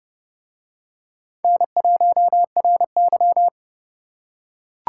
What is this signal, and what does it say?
Morse code, heard as a keyed tone, about 30 wpm, 700 Hz. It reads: N1RY